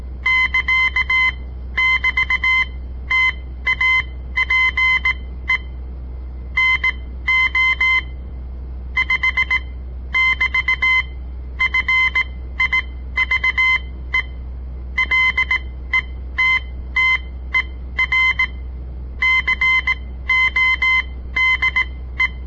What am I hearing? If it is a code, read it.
Morse code, heard as a keyed tone, about 17 wpm, spaced as for 13 wpm, 2050 Hz. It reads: <KA><BT>TAPE NO 5<BT>FIVE LETTER CODE
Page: On the air Morse code practice from the Southern Montana Amatuer Radio Association; all code, no read back